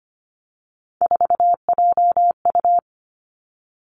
Morse code, heard as a keyed tone, about 25 wpm, 700 Hz. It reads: 4JU
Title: Three Challenging Letter-Number Combinations 25wpm